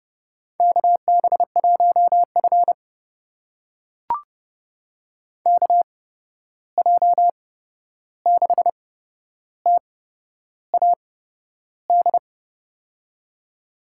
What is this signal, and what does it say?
Morse code, heard as a keyed tone, about 30 wpm, spaced as for 11 wpm, 700 Hz. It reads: KB1F KJ6TAD
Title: USA Call Signs - 8x Character Spacing - 30wpm